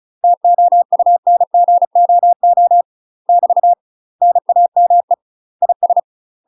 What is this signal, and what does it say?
Morse code, heard as a keyed tone, about 36 wpm, 700 Hz. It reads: TOUNGOO <BT> NAME IS
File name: UK MASTER QSOs TRAINING / 35WPM / UK MASTER QSOs 0029